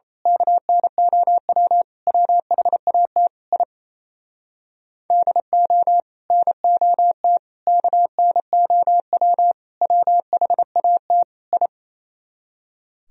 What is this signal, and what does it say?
Morse code, heard as a keyed tone, about 30 wpm, 700 Hz. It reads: KNOW WHAT I DO NOT KNOW WHAT I
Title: Speed-Racing - Sets of 5 Words 28wpm